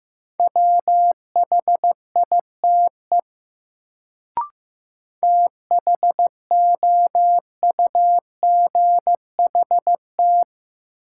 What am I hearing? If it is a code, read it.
Morse code, heard as a keyed tone, about 15 wpm, 700 Hz. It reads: WHITE THOUGHT
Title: Sets of 2 Words - New Word Drill - 15wpm